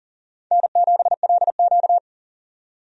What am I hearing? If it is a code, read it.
Morse code, heard as a keyed tone, about 40 wpm, 700 Hz. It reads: N7LQ
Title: USA Call Signs - 4 Characters - 40wpm